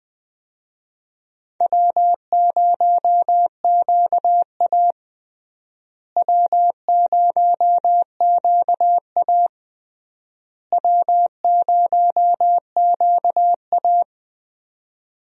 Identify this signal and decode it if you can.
Morse code, heard as a keyed tone, about 20 wpm, 700 Hz. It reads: W0QA W0QA W0QA
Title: USA Call Signs - 4 Characters - Repeated 3x - 20wpm